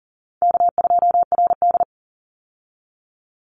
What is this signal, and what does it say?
Morse code, heard as a keyed tone, about 40 wpm, 700 Hz. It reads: K2RD
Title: Easy Call Signs Repeated 2x - 40wpm